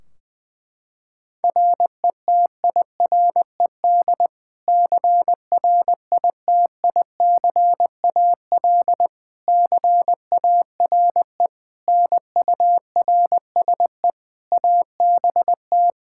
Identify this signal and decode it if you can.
Morse code, heard as a keyed tone, about 21 wpm, 700 Hz. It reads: RETIRED CRITICAL CARE NURSE ABT
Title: Practice QSO - Exchange 3 - More Info - 20wpm